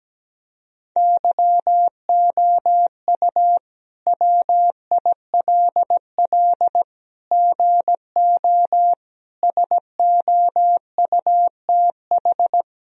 Morse code, heard as a keyed tone, about 17 wpm, 700 Hz. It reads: YOU WILL GO SOUTH
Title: Sentences from Top 1000 Words - Review Part 2 - 17wpm